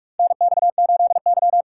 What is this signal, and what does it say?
Morse code, heard as a keyed tone, about 46 wpm, 700 Hz. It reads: NX8Y
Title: USA Call Signs - 4 Characters - 2x Character Spacing - 45wpm